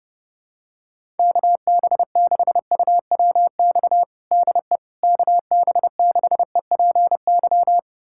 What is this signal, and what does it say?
Morse code, heard as a keyed tone, about 30 wpm, 700 Hz. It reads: KB6UWX DE KB6EPY
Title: USA Callsign DE Callsign 30wpm